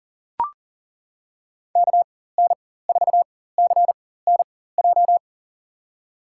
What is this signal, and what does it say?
Morse code, heard as a keyed tone, about 40 wpm, spaced as for 19 wpm, 700 Hz. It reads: KN4CNJ
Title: USA Call Signs - 6 Characters - 4x Character Spacing - 40wpm